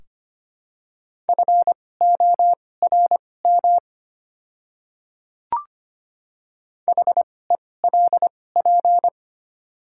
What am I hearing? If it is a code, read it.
Morse code, heard as a keyed tone, about 25 wpm, spaced as for 18 wpm, 700 Hz. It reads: FORM HELP
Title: 2x Character Spacing - Top 500 Words - 25wpm